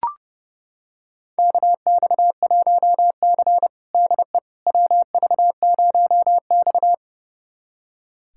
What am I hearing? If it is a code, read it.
Morse code, heard as a keyed tone, about 30 wpm, 700 Hz. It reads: KX1C DE WV0X
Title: USA Callsign DE Callsign - 4 Characters - 30wpm